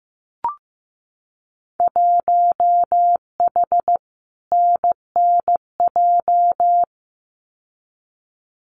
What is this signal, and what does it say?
Morse code, heard as a keyed tone, about 15 wpm, 700 Hz. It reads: 1H NNJ
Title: Winter Field Day Exchanges 15wpm